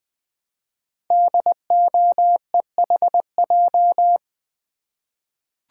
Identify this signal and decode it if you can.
Morse code, heard as a keyed tone, about 20 wpm, 700 Hz. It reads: DOEHJ
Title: Code Groups - 5 Letters - 20wpm